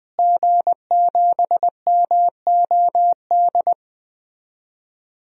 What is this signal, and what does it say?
Morse code, heard as a keyed tone, about 20 wpm, 700 Hz. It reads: G7MOD
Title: DXCC-100 Call Signs - 5 Characters - Repeated 3x - 20wpm